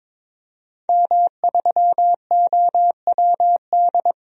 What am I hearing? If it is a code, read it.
Morse code, heard as a keyed tone, about 22 wpm, 700 Hz. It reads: M3OWD
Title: DXCC-100 Call Signs - 5 Characters - 22wpm